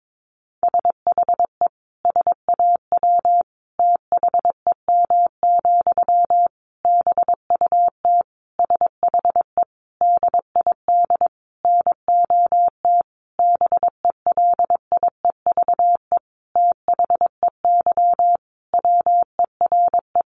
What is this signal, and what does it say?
Morse code, heard as a keyed tone, about 22 wpm, 700 Hz. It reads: SHE SAW THEM, BUT SHE DID NOT BELIEVE THEY WERE
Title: Sentences from Top 1000 Words 22wpm